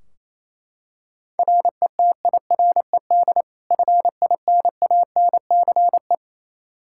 Morse code, heard as a keyed tone, about 28 wpm, 700 Hz. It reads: RETIRED FINANCE
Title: K1USN SST QSO Practice 28wpm